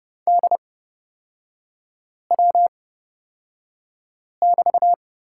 Morse code, heard as a keyed tone, about 30 wpm, spaced as for 5 wpm, 700 Hz. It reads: DW<BT>
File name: Lesson 38e - Letter q - Three Characters 30wpm